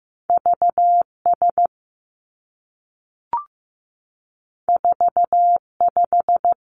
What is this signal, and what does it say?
Morse code, heard as a keyed tone, about 15 wpm, 700 Hz. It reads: V S 4 5
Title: Two Challenging Letter-Number Combinations 15wpm